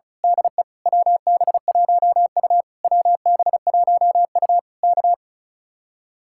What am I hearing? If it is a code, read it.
Morse code, heard as a keyed tone, about 32 wpm, 700 Hz. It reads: DE WB1U WB1U K
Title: Speed-Racing Common QSO Phrases - 35wpm